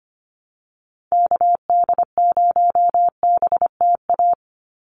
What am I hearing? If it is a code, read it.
Morse code, heard as a keyed tone, about 25 wpm, 690 Hz. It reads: KD0BTA
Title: USA Call Signs 25wpm